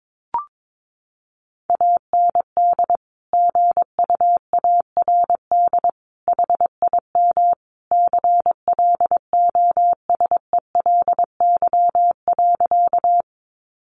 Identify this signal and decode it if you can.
Morse code, heard as a keyed tone, about 22 wpm, 700 Hz. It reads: AND GUARD HIM CLOSELY.
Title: Sentences from Top 1400 Words 22wpm